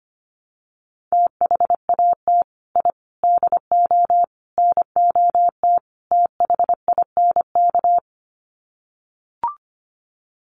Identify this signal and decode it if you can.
Morse code, heard as a keyed tone, about 25 wpm, 700 Hz. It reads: THAT I DO NOT THINK
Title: Sets of 5 Words Encore 25wpm